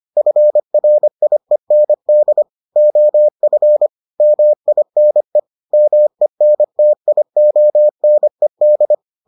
Morse code, heard as a keyed tone, about 25 wpm, 600 Hz. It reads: FRIEND OF MINE MENTIONED